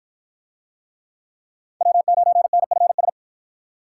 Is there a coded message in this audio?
Yes: Morse code from 1.8 to 3.1 s, heard as a keyed tone, about 48 wpm, 700 Hz. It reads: W9NFS